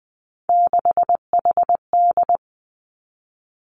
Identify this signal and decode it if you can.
Morse code, heard as a keyed tone, about 20 wpm, 700 Hz. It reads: 6HD
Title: Three Challenging Letter-Number Combinations 20wpm